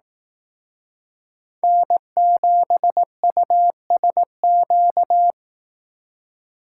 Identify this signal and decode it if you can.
Morse code, heard as a keyed tone, about 18 wpm, 700 Hz. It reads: N7USQ
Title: Speed Racing - USA 1x3 Call Signs 15wpm